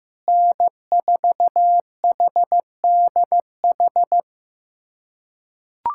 Morse code, heard as a keyed tone, about 15 wpm, 700 Hz. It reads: N4HDH
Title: USA Callsign DE Callsign - 5 Characters - 15wpm